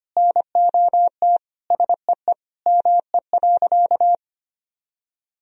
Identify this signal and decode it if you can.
Morse code, heard as a keyed tone, about 25 wpm, 700 Hz. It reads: NOT SEE ME.
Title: Sentences from Top 1000 Words - Review Part 1 - 25wpm